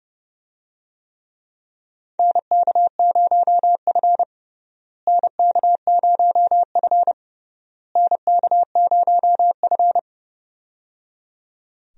Morse code, heard as a keyed tone, about 30 wpm, 700 Hz. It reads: NK0F NK0F NK0F
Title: USA Call Signs - 4 Characters - Repeated 3x - 30wpm